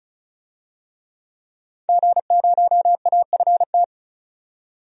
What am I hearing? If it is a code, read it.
Morse code, heard as a keyed tone, about 36 wpm, 700 Hz. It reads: G0AFT
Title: DXCC-100 Call Signs - 5 Characters - 35wpm